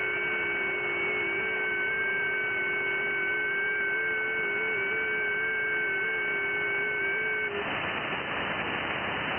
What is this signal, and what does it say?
FT8.
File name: IC-7100